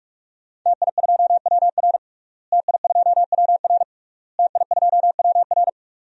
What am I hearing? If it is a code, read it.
Morse code, heard as a keyed tone, about 46 wpm, 700 Hz. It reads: TI2WR TI2WR TI2WR
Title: DXCC-100 Call Signs - 5 Characters - Repeated 3x - 45wpm